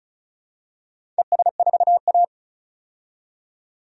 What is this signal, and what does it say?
Morse code, heard as a keyed tone, about 34 wpm, 700 Hz. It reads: ES4A
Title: DXCC-100 Call Signs - 4 Characters - 35wpm